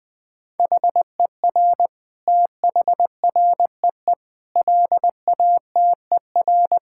Morse code, heard as a keyed tone, about 20 wpm, 700 Hz. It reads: HER THREE LATER
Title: Sentences from Top 1000 Words - Review Part 6 - 20wpm